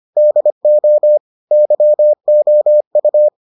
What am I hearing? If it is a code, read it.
Morse code, heard as a keyed tone, about 25 wpm, 600 Hz. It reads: DO YOU